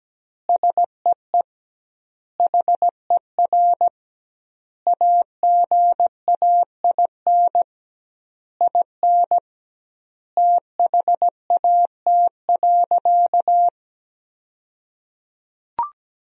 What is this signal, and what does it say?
Morse code, heard as a keyed tone, about 17 wpm, 700 Hz. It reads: SEE HER AGAIN IN THAT.
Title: 2x Word-Spacing - Sentences from Top 200 Words - 17wpm